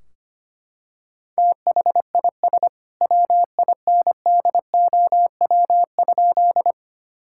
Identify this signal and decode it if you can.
Morse code, heard as a keyed tone, about 25 wpm, 700 Hz. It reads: THIS WINDOW?
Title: Sentences from Top 1600 Words 25wpm